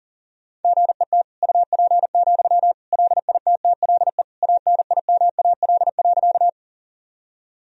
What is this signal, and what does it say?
Morse code, heard as a keyed tone, about 40 wpm, 700 Hz. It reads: GET UP, LITTLE ANIMAL.
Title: Sentences from Top 1000 Words - Review Part 1 - 40wpm